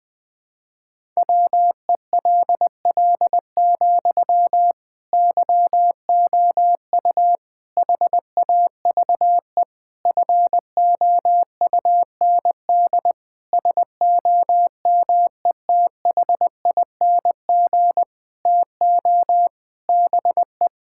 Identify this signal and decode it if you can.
Morse code, heard as a keyed tone, about 20 wpm, 700 Hz. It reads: WELL, YOU HAVE FOUND SOMETHING TO BE
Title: Sentences from Top 1000 Words - Review Part 6 - 20wpm